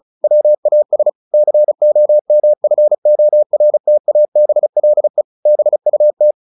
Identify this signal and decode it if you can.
Morse code, heard as a keyed tone, about 34 wpm, 600 Hz. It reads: WAS COMFORTABLE BUT
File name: the old man and the sea / the old man and the sea 1 35wpm